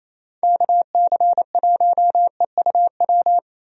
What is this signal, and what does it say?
Morse code, heard as a keyed tone, about 27 wpm, 700 Hz. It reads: KC1EUW
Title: USA Call Signs - 6 Characters - Repeated 3x - 28wpm